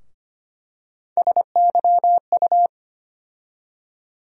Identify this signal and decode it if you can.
Morse code, heard as a keyed tone, about 25 wpm, 700 Hz. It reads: SYU
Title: USA Call Sign Postfixes - 3 Letters 25wpm